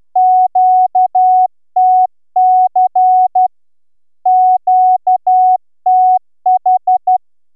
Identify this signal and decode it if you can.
Morse code, heard as a keyed tone, about 11 wpm, 740 Hz. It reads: QTC QTH